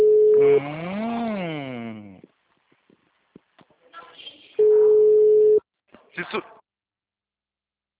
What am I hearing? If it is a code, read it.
Ringback tone (Europe, 425 Hz).
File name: out- -2002- - - .5483